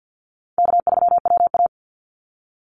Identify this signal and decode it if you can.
Morse code, heard as a keyed tone, about 50 wpm, 700 Hz. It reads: K3WA
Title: Easy Call Signs Repeated 2x - 50wpm